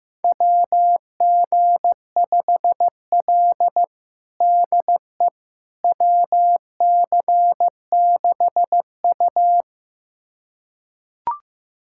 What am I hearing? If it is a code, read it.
Morse code, heard as a keyed tone, about 15 wpm, 700 Hz. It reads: WG5L DE WC6U
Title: USA Callsign DE Callsign - 4 Characters - 15wpm